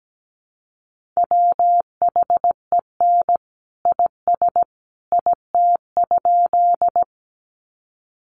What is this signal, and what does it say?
Morse code, heard as a keyed tone, about 17 wpm, 700 Hz. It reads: WHEN IS IT?
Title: Sentences from Top 100 Words 17wpm